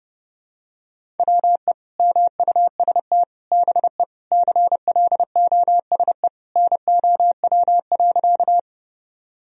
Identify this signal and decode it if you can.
Morse code, heard as a keyed tone, about 30 wpm, 700 Hz. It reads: WE MUST BE CLOSE NOW.
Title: Sentences from Top 1000 Words - Review Part 1 - 30wpm